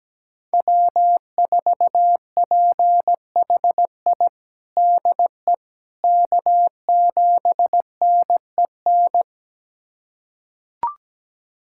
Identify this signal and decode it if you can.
Morse code, heard as a keyed tone, about 17 wpm, 700 Hz. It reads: W4PHI DE K7NEN
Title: USA Callsign DE Callsign - 5 Characters - 17wpm